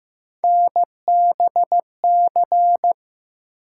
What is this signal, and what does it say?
Morse code, heard as a keyed tone, about 15 wpm, 700 Hz. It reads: NBC
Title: Sets of 5 Words Encore - New Word Drill - 15wpm